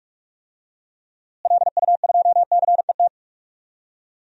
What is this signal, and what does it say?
Morse code, heard as a keyed tone, about 43 wpm, 700 Hz. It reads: LU2CET
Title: DXCC-100 Call Signs - 6 Characters - Repeated 2x - 45wpm